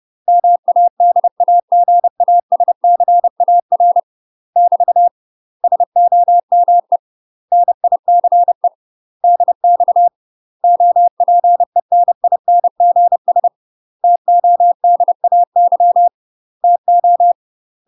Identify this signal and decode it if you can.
Morse code, heard as a keyed tone, about 30 wpm, 700 Hz. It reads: MADAGASCAR <BT> SOME NICE DX OPENINGS TODAY TO
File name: UK MASTER QSOs TRAINING / 30WPM / UK MASTER QSOs 0034